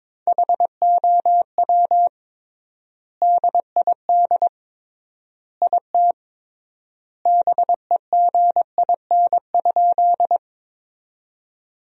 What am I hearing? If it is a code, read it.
Morse code, heard as a keyed tone, about 22 wpm, 700 Hz. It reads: HOW DID IT BEGIN?
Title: 3x Word-Spacing - Sentences from Top 300 Words - 22wpm